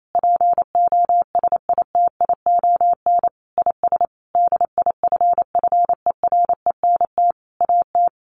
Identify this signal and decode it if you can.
Morse code, heard as a keyed tone, about 28 wpm, 700 Hz. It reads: POSITION IS DIFFERENT AT
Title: Sentences from Top 1400 Words 28wpm